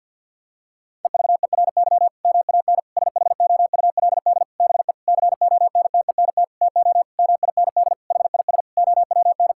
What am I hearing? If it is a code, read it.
Morse code, heard as a keyed tone, about 50 wpm, 700 Hz. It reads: EVERY MAN SHOULD BE CONTENT TO MIND HIS OWN
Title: Sentences from Top 1000 Words - Review Part 6 - 50wpm